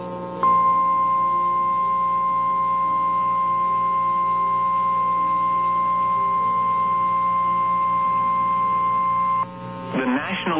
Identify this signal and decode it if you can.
Emergency Alert System (SAME) tones (the 1050 Hz attention tone), with speech. It is heard from 0.4 to 9.5 s.
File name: eas / KNCYKBIE / KNCYKBIE